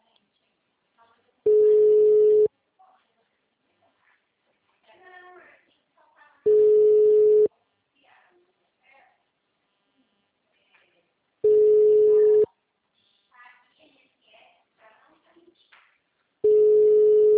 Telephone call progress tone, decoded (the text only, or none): special information tone